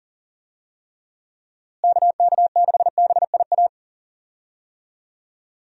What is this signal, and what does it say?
Morse code, heard as a keyed tone, about 40 wpm, 700 Hz. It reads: KK6BIA
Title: USA Call Signs - 6 Characters - Repeated 3x - 40wpm